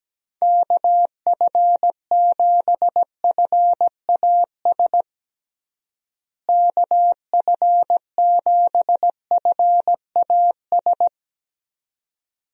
Morse code, heard as a keyed tone, about 17 wpm, 700 Hz. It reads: KF7FAS KF7FAS
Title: USA Call Signs - 6 Characters - Repeated 2x - 17wpm